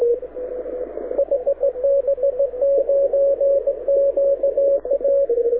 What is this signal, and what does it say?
Morse code, heard as a keyed tone, about 17 wpm, 560 Hz, with radio noise. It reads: HB9QA
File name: audio zawody / IARU 2015 CW SN7Q